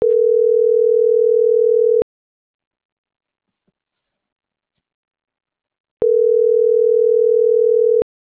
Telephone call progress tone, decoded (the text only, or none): ringback tone